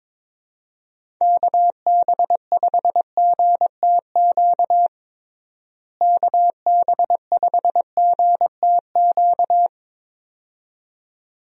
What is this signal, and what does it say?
Morse code, heard as a keyed tone, about 22 wpm, 700 Hz. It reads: KB5GTQ KB5GTQ
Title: USA Call Signs - 6 Characters - Repeated 2x - 22wpm